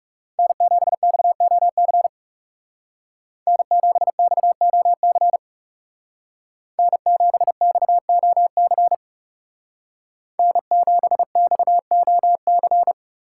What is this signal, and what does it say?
Morse code, heard as a keyed tone, about 35 wpm, 700 Hz. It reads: N7XOC N7XOC N7XOC N7XOC
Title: Speed Racing - USA 1x3 Call Signs 30wpm